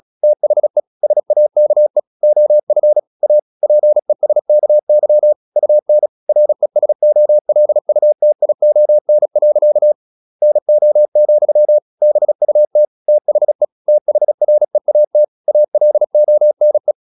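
Morse code, heard as a keyed tone, about 36 wpm, 600 Hz. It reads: THE SAKE OF A PESKY UN RESOLUTION. NO, BUT THE THREAT ALONE
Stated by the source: Morse practice at 36 wpm, text of Ken Follett's novel Never, from CW med Gnister.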